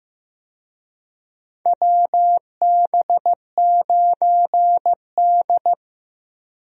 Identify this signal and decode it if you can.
Morse code, heard as a keyed tone, about 15 wpm, 700 Hz. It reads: WB9D
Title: USA Call Signs - 4 Characters - Repeated 2x - 15wpm